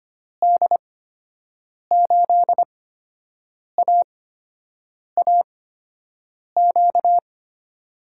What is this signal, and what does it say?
Morse code, heard as a keyed tone, about 25 wpm, spaced as for 7 wpm, 700 Hz. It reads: D8AAQ
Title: USA Call Signs - 6 Characters - 8x Character Spacing - 25wpm